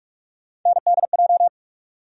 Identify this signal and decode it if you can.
Morse code, heard as a keyed tone, about 46 wpm, 700 Hz. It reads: NDJ
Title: USA Call Sign Postfixes - 3 Letters 45wpm